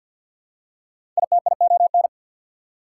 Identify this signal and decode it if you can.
Morse code, heard as a keyed tone, about 53 wpm, 690 Hz. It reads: ITION
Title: Prefixes and Suffixes 50wpm